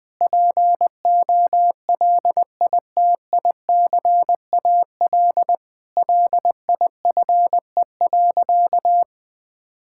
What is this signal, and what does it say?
Morse code, heard as a keyed tone, about 21 wpm, 700 Hz. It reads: POLITICAL LIFE.
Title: Sentences from Top 1000 Words - Review Part 5 - 20wpm